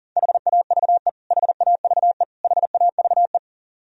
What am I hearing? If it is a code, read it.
Morse code, heard as a keyed tone, about 40 wpm, 700 Hz. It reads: HAVE HAVE HAVE
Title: Top 100 Words Repeated 40wpm